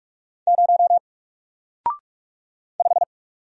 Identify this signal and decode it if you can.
Morse code, heard as a keyed tone, about 45 wpm, 700 Hz. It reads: 0 5
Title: Single Character (Letter:Number:Punctuation:Prosign) Alt 45wpm